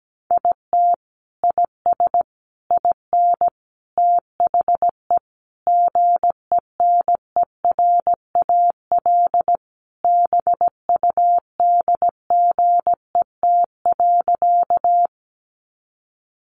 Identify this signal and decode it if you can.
Morse code, heard as a keyed tone, about 17 wpm, 700 Hz. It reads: IT IS IN THE GENERAL BUDGET.
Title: Sentences from Top 1200 Words 17wpm